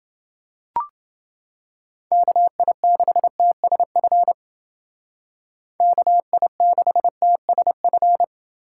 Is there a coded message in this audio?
Yes: Morse code from 2.1 to 8.2 s, heard as a keyed tone, about 29 wpm, 700 Hz. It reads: KI6TSF KI6TSF